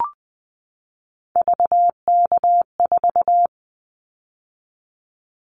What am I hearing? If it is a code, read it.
Morse code, heard as a keyed tone, about 20 wpm, 700 Hz. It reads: VK4
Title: Callsign Prefixes for Top 100 DXCC - 20wpm